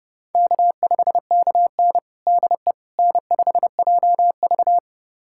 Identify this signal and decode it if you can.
Morse code, heard as a keyed tone, about 32 wpm, 700 Hz. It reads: K5KN DE N5JV
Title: USA Callsign DE Callsign - 4 Characters - 30wpm